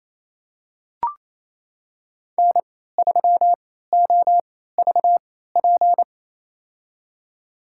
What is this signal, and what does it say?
Morse code, heard as a keyed tone, about 28 wpm, spaced as for 16 wpm, 700 Hz. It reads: N3OVP
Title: USA Call Signs - 3x Character Spacing - 28wpm